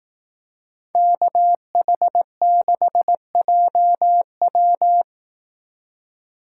Morse code, heard as a keyed tone, about 18 wpm, 700 Hz. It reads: KH6JW
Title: Speed Racing - USA 2x2 Call Signs 15wpm